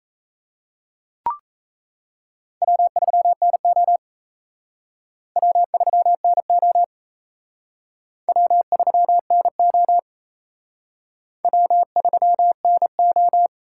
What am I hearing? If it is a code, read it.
Morse code, heard as a keyed tone, about 35 wpm, 700 Hz. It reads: W3NO W3NO W3NO W3NO
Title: Speed Racing - USA 1x2 Call Signs 28wpm